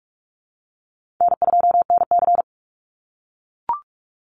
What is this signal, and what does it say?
Morse code, heard as a keyed tone, about 44 wpm, 700 Hz. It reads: N2NC
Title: Call Signs - 45wpm